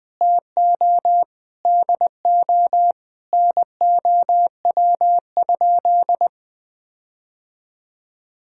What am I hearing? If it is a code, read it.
Morse code, heard as a keyed tone, about 20 wpm, 700 Hz. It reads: TO DO NOW?
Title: Sentences with 6-Word Phrases - Spoken Beginning - 20wpm